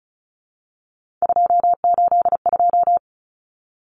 Morse code, heard as a keyed tone, about 34 wpm, 700 Hz. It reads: 282
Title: Numbers - 3-Digit 35wpm